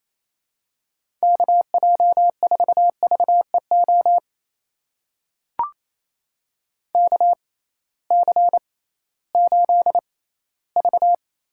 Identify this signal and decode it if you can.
Morse code, heard as a keyed tone, about 28 wpm, spaced as for 16 wpm, 700 Hz. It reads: KJ4VEO KC8V
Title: USA Call Signs - 6 Characters - 6x Character Spacing - 28wpm